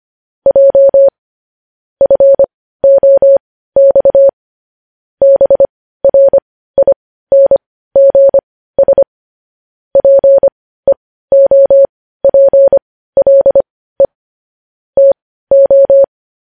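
Morse code, heard as a keyed tone, about 25 wpm, spaced as for 15 wpm, 550 Hz. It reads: J FOX BRINGS PEOPLE TO
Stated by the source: Morse code news headlines sent at 15 words per minute.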